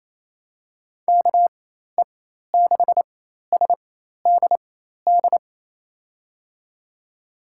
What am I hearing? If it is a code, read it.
Morse code, heard as a keyed tone, about 28 wpm, spaced as for 13 wpm, 700 Hz. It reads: KE6SDD
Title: USA Call Signs - 6 Characters - 4x Character Spacing - 28wpm